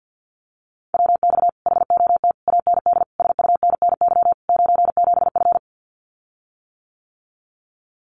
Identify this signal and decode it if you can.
Morse code, heard as a keyed tone, about 50 wpm, 700 Hz. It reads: WX HOT AND SUNNY 97F
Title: Common QSO Phrases Encore 50wpm